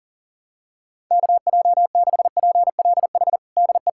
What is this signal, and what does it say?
Morse code, heard as a keyed tone, about 37 wpm, 700 Hz. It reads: KJ6PLH DE
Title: USA Callsign DE Callsign - 6 Characters - 40wpm